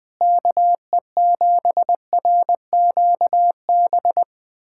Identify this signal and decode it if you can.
Morse code, heard as a keyed tone, about 20 wpm, 700 Hz. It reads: KE7RQB
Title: USA Callsign DE Callsign - 6 Characters - 20wpm